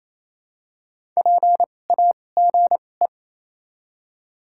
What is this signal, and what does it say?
Morse code, heard as a keyed tone, about 28 wpm, spaced as for 20 wpm, 700 Hz. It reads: PAGE
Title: 2x Character Spacing - Top 300 Words - 28wpm